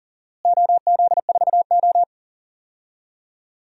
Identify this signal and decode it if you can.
Morse code, heard as a keyed tone, about 39 wpm, 700 Hz. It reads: OZ4O
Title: DXCC-100 Call Signs - 4 Characters - Repeated 2x - 40wpm